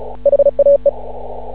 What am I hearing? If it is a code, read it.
Morse code, heard as a keyed tone, about 36 wpm, 560 Hz. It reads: HAE
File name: CW / ha6oi 2